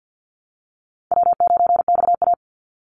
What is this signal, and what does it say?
Morse code, heard as a keyed tone, about 50 wpm, 700 Hz. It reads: W9XA
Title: Easy Call Signs - 50wpm